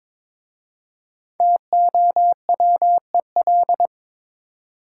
Morse code, heard as a keyed tone, about 22 wpm, 700 Hz. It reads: TOWEL
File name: 5 Letter Words - First 3 Letters Given - 22wpm